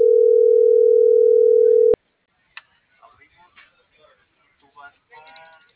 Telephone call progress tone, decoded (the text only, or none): ringback tone